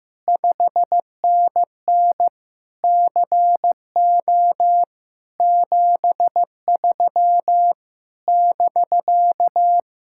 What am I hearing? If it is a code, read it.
Morse code, heard as a keyed tone, about 15 wpm, 700 Hz. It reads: 5NN CO 73 <BK>
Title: Speed-Racing Common QSO Phrases Encore - 15wpm